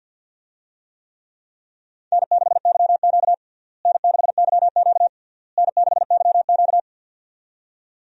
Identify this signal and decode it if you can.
Morse code, heard as a keyed tone, about 51 wpm, 700 Hz. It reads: N6YQ N6YQ N6YQ
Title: USA Call Signs - 4 Characters - Repeated 3x - 50wpm